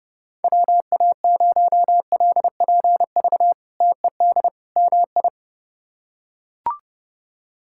Morse code, heard as a keyed tone, about 31 wpm, 700 Hz. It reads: WA0LPV TED MI
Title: K1USN SST QSO Practice - 30wpm